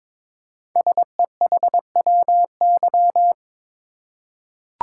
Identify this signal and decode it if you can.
Morse code, heard as a keyed tone, about 22 wpm, 700 Hz. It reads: SEHWY
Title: Code Groups - 5 Letters - 22wpm